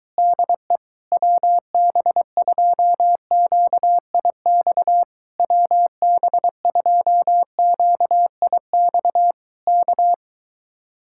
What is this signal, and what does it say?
Morse code, heard as a keyed tone, about 23 wpm, 700 Hz. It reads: DE WB2QIX WB2QIX K